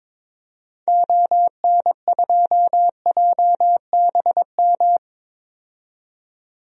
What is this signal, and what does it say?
Morse code, heard as a keyed tone, about 22 wpm, 700 Hz. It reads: ON2JBM
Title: DXCC-100 Call Signs - 6 Characters - 22wpm